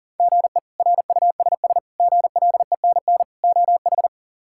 Morse code, heard as a keyed tone, about 40 wpm, 700 Hz. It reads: GE RUSS GLENN OH
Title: K1USN SST QSO Practice - 40wpm